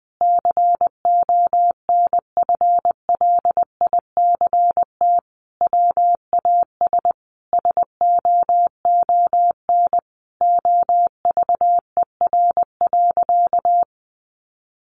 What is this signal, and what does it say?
Morse code, heard as a keyed tone, about 20 wpm, 700 Hz. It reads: CONFLICT WAS SOON OVER.
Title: Sentences from Top 1300 Words 20wpm